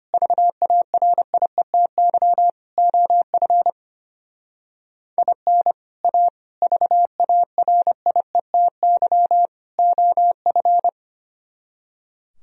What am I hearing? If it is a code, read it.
Morse code, heard as a keyed tone, about 31 wpm, 700 Hz. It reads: VARIETY OF IN A VARIETY OF
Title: Speed-Racing - Sets of 4 Words 25wpm